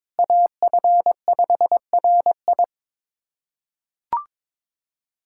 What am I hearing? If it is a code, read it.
Morse code, heard as a keyed tone, about 22 wpm, 700 Hz. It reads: AF5RI
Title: USA Callsign DE Callsign - 5 Characters - 22wpm